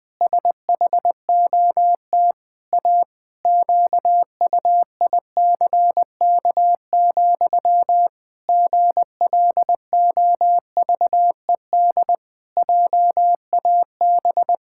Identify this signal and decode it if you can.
Morse code, heard as a keyed tone, about 20 wpm, 700 Hz. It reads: SHOT A QUICK, GLOVED JAB
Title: Pangram Challenge v2 - 20wpm